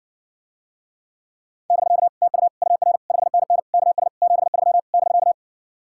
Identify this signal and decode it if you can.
Morse code, heard as a keyed tone, about 60 wpm, 700 Hz. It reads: <BK> TU UR 5NN KS 73 <BK>